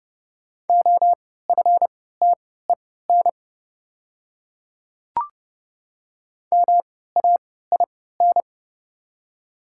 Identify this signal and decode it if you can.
Morse code, heard as a keyed tone, about 30 wpm, spaced as for 17 wpm, 700 Hz. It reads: OFTEN MAIN
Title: 3x Character Spacing - Top 500 Words - 30wpm